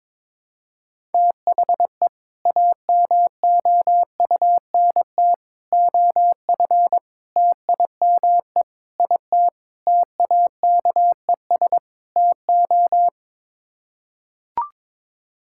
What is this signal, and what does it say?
Morse code, heard as a keyed tone, about 22 wpm, 700 Hz. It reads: THE AMOUNT OF TIME IT TAKES TO
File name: practice / 22wpm / Sets of 7 Words - 22wpm